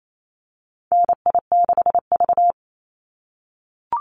Morse code, heard as a keyed tone, about 28 wpm, 700 Hz. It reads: NI6V
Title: Call Signs - 28wpm